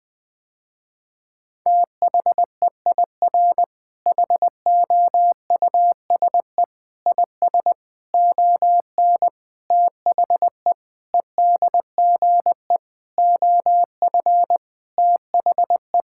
Morse code, heard as a keyed tone, about 20 wpm, 700 Hz. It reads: THEIR HOUSE IS ON THE EDGE OF THE
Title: Sentences with 5-Word Phrases - Spoken Beginning - 20wpm